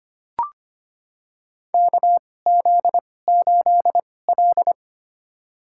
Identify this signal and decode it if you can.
Morse code, heard as a keyed tone, about 25 wpm, spaced as for 18 wpm, 700 Hz. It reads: KZ8L
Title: USA Call Signs - 4 Characters - 2x Character Spacing - 25wpm